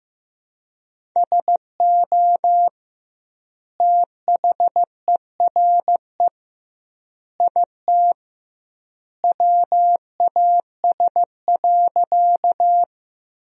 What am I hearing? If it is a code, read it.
Morse code, heard as a keyed tone, about 15 wpm, 700 Hz. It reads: SO THERE IT WAS.
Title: 2x Word-Spacing - Sentences from Top 100 Words - 15wpm